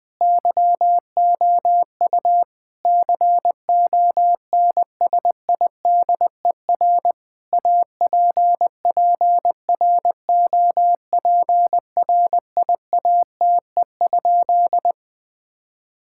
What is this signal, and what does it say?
Morse code, heard as a keyed tone, about 20 wpm, 700 Hz. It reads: YOU CONSIDER APPROPRIATE?
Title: Sentences from Top 1000 Words - Review Part 5 - 20wpm